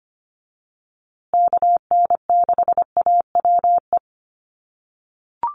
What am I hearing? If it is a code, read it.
Morse code, heard as a keyed tone, about 25 wpm, 700 Hz. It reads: KN6AWE
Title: USA Call Signs Repeated 3x 25wpm